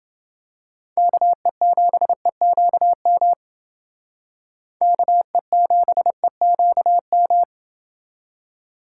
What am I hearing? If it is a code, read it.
Morse code, heard as a keyed tone, about 29 wpm, 700 Hz. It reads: KE7EQM KE7EQM